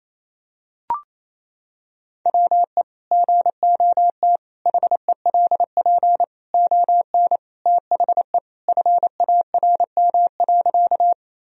Morse code, heard as a keyed tone, about 28 wpm, 690 Hz. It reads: WE GOT HELP ON THE FARM.
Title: Sentences from Top 1000 Words - Review Part 2 - 28wpm